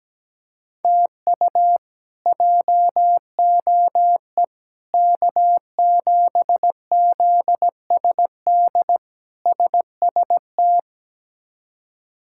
Morse code, heard as a keyed tone, about 17 wpm, 700 Hz. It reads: TU JOE K7ZSD SST
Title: K1USN SST QSO Practice - 17wpm